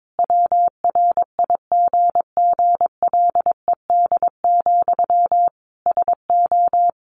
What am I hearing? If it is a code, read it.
Morse code, heard as a keyed tone, about 22 wpm, 700 Hz. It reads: WRIGGLED, SO
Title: A Sound Of Thunder 22wpm